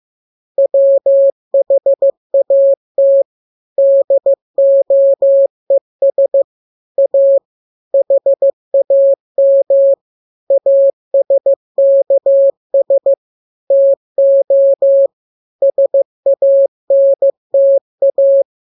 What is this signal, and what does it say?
Morse code, heard as a keyed tone, about 15 wpm, 550 Hz. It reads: WHAT DOES A HAM ASKS TO SANTA